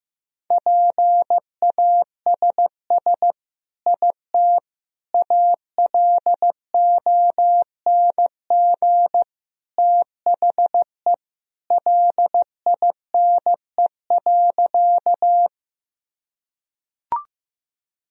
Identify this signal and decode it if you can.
Morse code, heard as a keyed tone, about 16 wpm, 700 Hz. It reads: PASS IT ALONG THE LINE.
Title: Sentences from Top 1000 Words - Review Part 1 - 15wpm